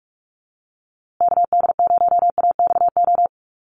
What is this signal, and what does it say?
Morse code, heard as a keyed tone, about 44 wpm, 700 Hz. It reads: KD0AXO
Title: USA Call Signs Repeated 2x 45wpm